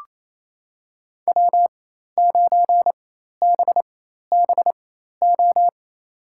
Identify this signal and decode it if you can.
Morse code, heard as a keyed tone, about 28 wpm, spaced as for 13 wpm, 700 Hz. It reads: W9BBO
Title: USA Call Signs - 5 Characters - 4x Character Spacing - 28wpm